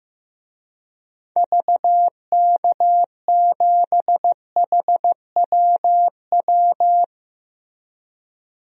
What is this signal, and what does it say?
Morse code, heard as a keyed tone, about 15 wpm, 700 Hz. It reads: VK7HWW
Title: DXCC-100 Call Signs - 6 Characters - Repeated 3x - 15wpm